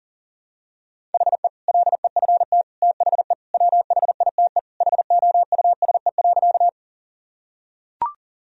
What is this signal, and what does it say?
Morse code, heard as a keyed tone, about 40 wpm, 700 Hz. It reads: HE LEFT THE WHITE HOUSE.
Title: Sentences from Top 1000 Words - Review Part 1 - 40wpm